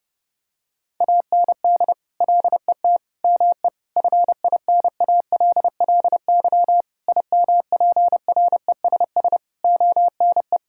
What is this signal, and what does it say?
Morse code, heard as a keyed tone, about 30 wpm, 700 Hz. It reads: AND LET ME FINALLY IMPRESS ONE
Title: Sentences from Top 1600 Words 30wpm